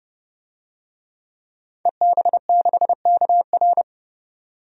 Morse code, heard as a keyed tone, about 30 wpm, 700 Hz. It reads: EB6KR
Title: DXCC-100 Call Signs - 5 Characters - 30wpm